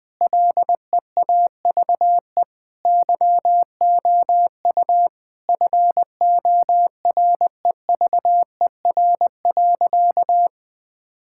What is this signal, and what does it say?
Morse code, heard as a keyed tone, about 20 wpm, 700 Hz. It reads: LEAVE YOU FOREVER.
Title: Sentences from Top 1600 Words 20wpm